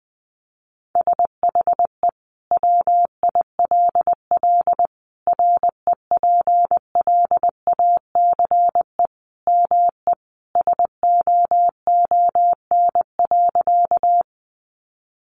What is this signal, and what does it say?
Morse code, heard as a keyed tone, about 20 wpm, 700 Hz. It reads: SHE WILL REPLACE ME SOON.
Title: Sentences from Top 1400 Words 20wpm